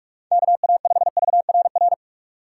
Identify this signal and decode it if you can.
Morse code, heard as a keyed tone, about 42 wpm, 700 Hz. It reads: KA5VRR
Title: USA Call Signs - 6 Characters - 4x Character Spacing - 45wpm